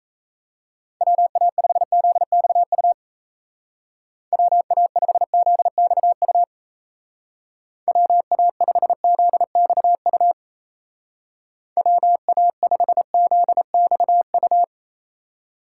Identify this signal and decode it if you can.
Morse code, heard as a keyed tone, about 37 wpm, 700 Hz. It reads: WA5ZXU WA5ZXU WA5ZXU WA5ZXU